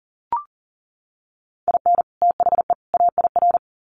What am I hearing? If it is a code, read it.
Morse code, heard as a keyed tone, about 40 wpm, 700 Hz. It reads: IN THE AIR
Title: Sets of 3 Words 40wpm